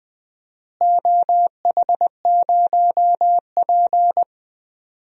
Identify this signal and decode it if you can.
Morse code, heard as a keyed tone, about 20 wpm, 700 Hz. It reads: OH0P
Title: DXCC-100 Call Signs - 4 Characters - Repeated 2x - 20wpm